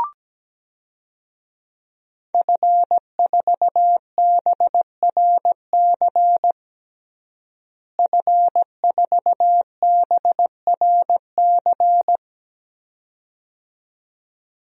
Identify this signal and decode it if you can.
Morse code, heard as a keyed tone, about 17 wpm, 700 Hz. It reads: F4BRC F4BRC
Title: DXCC-100 Call Signs - 5 Characters - Repeated 2x - 17wpm